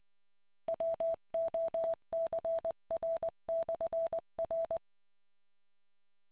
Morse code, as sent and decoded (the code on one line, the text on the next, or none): .-- --- -.-. .-. -..-. .-.
WOCR/R